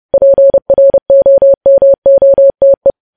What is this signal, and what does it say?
Morse code, heard as a keyed tone, about 30 wpm, 550 Hz. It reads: PROMOTE
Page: news headlines in morse code at 30 wpm